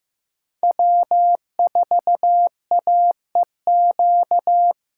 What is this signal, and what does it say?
Morse code, heard as a keyed tone, about 15 wpm, 700 Hz. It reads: W4AEQ
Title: USA Call Signs - 5 Characters - 6x Character Spacing - 15wpm